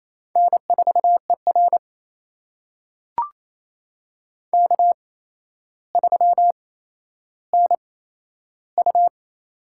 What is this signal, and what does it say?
Morse code, heard as a keyed tone, about 28 wpm, spaced as for 12 wpm, 700 Hz. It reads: N4ER K3NU
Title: USA Call Signs - 4 Characters - 8x Character Spacing - 28wpm